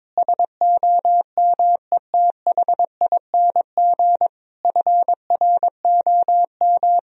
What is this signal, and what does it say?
Morse code, heard as a keyed tone, about 22 wpm, 700 Hz. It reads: SOMETHING FROM
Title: Sentences from Top 1000 Words - Review Part 4 - 22wpm